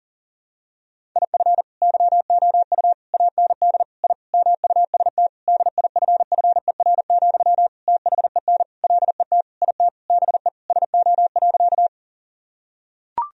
Morse code, heard as a keyed tone, about 40 wpm, 700 Hz. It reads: IF YOU AND I MUST DIFFER, THEN LET IT BE SO.
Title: Sentences from Top 1000 Words - Review Part 1 - 40wpm